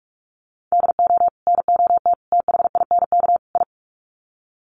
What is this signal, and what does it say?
Morse code, heard as a keyed tone, about 44 wpm, 700 Hz. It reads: DO NOT THINK I
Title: Sets of 4 Words 45wpm